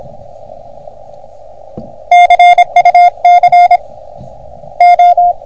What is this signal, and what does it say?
Morse code, heard as a keyed tone, about 26 wpm, 720 Hz. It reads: CUC O
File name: OM3PA PILUP CUC 388